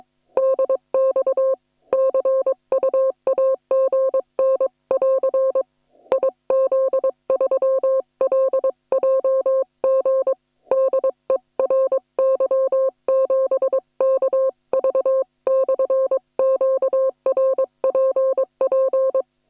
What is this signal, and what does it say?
Morse code, heard as a keyed tone, about 22 wpm, 530 Hz. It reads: DX CU AGN<AR> IZ3LJG DE RY7KV/QRPP
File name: attachments / hf 1 001